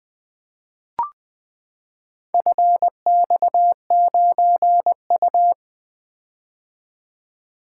Morse code, heard as a keyed tone, about 20 wpm, 700 Hz. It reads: FX9U
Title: Four Letter-Number Combinations - 20wpm